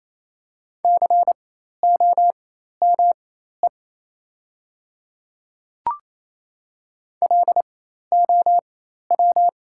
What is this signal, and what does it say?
Morse code, heard as a keyed tone, about 28 wpm, spaced as for 13 wpm, 690 Hz. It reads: COME LOW
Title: 4x Character Spacing - Top 200 Words - 28wpm